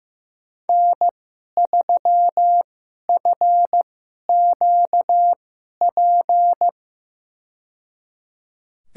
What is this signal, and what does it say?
Morse code, heard as a keyed tone, about 15 wpm, spaced as for 11 wpm, 700 Hz. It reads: N3FQP
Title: USA Call Signs - 5 Characters - 2x Character Spacing - 15wpm